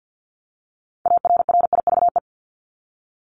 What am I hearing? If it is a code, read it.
Morse code, heard as a keyed tone, about 50 wpm, 700 Hz. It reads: ARRIVE
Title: ED Suffix - Top 100 Words with Base Words 50wpm